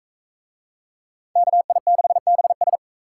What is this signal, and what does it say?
Morse code, heard as a keyed tone, about 41 wpm, 700 Hz. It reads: KI6BS